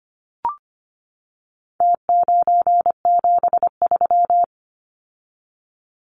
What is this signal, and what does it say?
Morse code, heard as a keyed tone, about 25 wpm, 700 Hz. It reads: T973
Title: Numbers - Contest Style - 4-Digit 25wpm